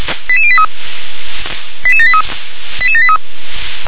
Five-tone selective call (ZVEI), heard as quick runs of five tones, tones of about 70 ms. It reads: 88083 88783 88783